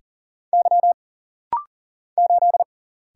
Morse code, heard as a keyed tone, about 40 wpm, 700 Hz. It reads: Y 8
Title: Single Letter-Number Alt - Rapid-Fire 40wpm